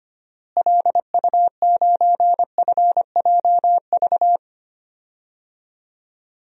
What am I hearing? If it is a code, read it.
Morse code, heard as a keyed tone, about 25 wpm, 700 Hz. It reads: LU9FJV
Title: DXCC-100 Call Signs - 6 Characters - Repeated 3x - 25wpm